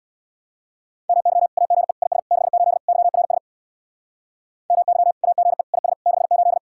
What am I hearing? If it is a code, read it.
Morse code, heard as a keyed tone, about 61 wpm, 700 Hz. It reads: MY AGE IS 68 YRS MY AGE IS 68
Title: Speed-Racing Common QSO Phrases - 50wpm